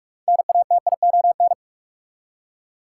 Morse code, heard as a keyed tone, about 47 wpm, 700 Hz. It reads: NATION
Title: Six Letter Words - 45wpm